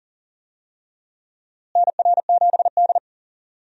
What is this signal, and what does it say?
Morse code, heard as a keyed tone, about 37 wpm, 700 Hz. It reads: NR7D
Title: USA Call Signs - 4 Characters - Repeated 3x - 40wpm